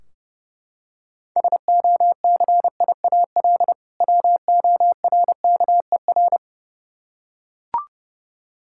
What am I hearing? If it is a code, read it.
Morse code, heard as a keyed tone, about 32 wpm, 700 Hz. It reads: SOCIAL WORKER
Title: Practice QSO - Exchange 3 - More Info - 30wpm